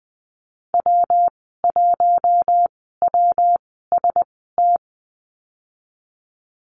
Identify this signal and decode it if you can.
Morse code, heard as a keyed tone, about 20 wpm, spaced as for 14 wpm, 700 Hz. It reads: W1WST
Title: 167 Lesson 27g - Number 7 - Call Signs 20wpm